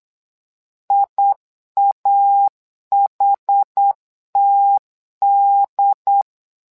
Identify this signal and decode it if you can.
Morse code, heard as a keyed tone, about 8 wpm, 800 Hz. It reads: IAHTD
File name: LEZIONI / 2 UHDN / 2 UHD 8wpm 2258